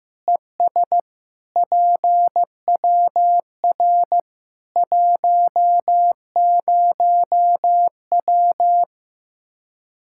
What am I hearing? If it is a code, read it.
Morse code, heard as a keyed tone, about 15 wpm, 700 Hz. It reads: ES PWR 10W
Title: K1USN SST QSO Practice 15wpm